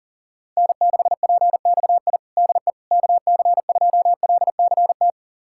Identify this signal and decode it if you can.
Morse code, heard as a keyed tone, about 40 wpm, 700 Hz. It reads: N6PXI DE KC2LCT
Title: USA Callsign DE Callsign 40wpm